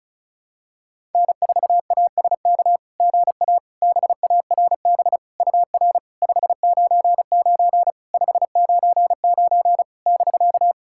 Morse code, heard as a keyed tone, about 36 wpm, 690 Hz. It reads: N4ASK GA BARB UR 599 599 <BK>
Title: SOTA QSO Practice 35wpm